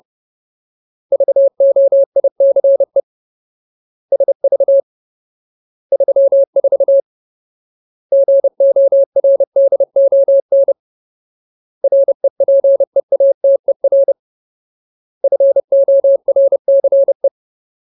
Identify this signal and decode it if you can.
Morse code, heard as a keyed tone, about 30 wpm, 550 Hz. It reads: VOICE SV 34 GORDON REPEATER FORCE
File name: Ultimate Word List / 30WPM Ultimate